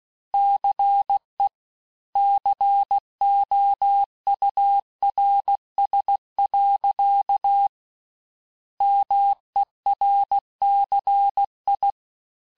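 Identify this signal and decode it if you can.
Morse code, heard as a keyed tone, about 16 wpm, 800 Hz. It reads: CE COURS. MERCI
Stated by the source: Morse course text, end of QSO, sent at 15 words per minute.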